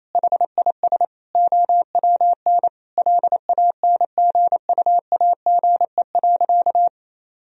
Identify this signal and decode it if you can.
Morse code, heard as a keyed tone, about 28 wpm, 700 Hz. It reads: HIS OWN LANGUAGE.
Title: Sentences from Top 1000 Words - Review Part 3 - 28wpm